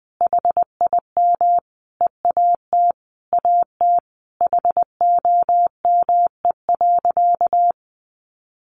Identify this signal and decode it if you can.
Morse code, heard as a keyed tone, about 20 wpm, 700 Hz. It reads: HIM EAT AT HOME.
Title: Sentences from Top 1400 Words 20wpm